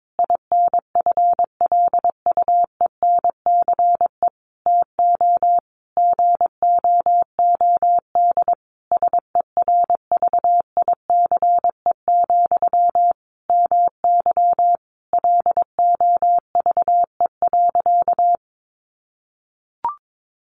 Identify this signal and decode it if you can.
Morse code, heard as a keyed tone, about 22 wpm, 700 Hz. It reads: INFLUENCE TO GOOD SERVICE, MY LOVE.
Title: Sentences from Top 1100 Words 22wpm